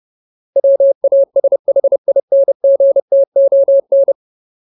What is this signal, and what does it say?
Morse code, heard as a keyed tone, about 30 wpm, 550 Hz. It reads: WASHINGTON
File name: Ultimate Word List / 30WPM Ultimate